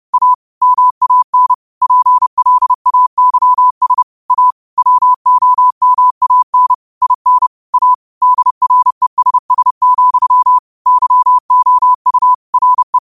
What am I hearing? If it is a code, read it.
Morse code, heard as a keyed tone, about 30 wpm, 1000 Hz. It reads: A MAN PLAYS A WOMAN IN A DRESS, YOU RE
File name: qotdfiles / 30wpm